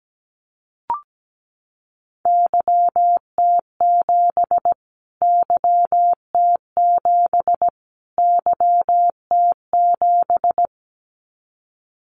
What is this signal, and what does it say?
Morse code, heard as a keyed tone, about 17 wpm, 700 Hz. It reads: YT7 YT7 YT7
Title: Callsign Prefixes for Top 25 DXCC - Repeated 3x - 17wpm